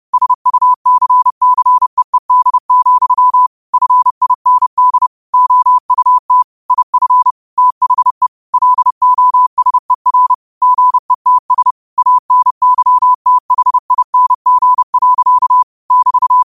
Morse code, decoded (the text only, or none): SUCCEED, FIND OUT IF THE LOSER GETS ANYTHING. <BT>